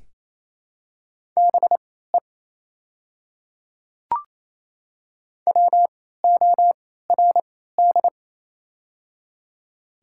Morse code, heard as a keyed tone, about 28 wpm, spaced as for 16 wpm, 700 Hz. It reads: BE WORD
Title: 3x Character Spacing - Top 300 Words - 28wpm